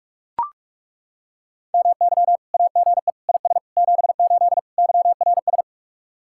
Morse code, heard as a keyed tone, about 38 wpm, 700 Hz. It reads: MY AGE IS 78 YRS